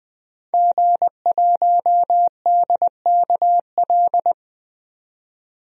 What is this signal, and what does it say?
Morse code, heard as a keyed tone, about 20 wpm, 700 Hz. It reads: G1DKL
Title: DXCC-100 Call Signs - 5 Characters - 20wpm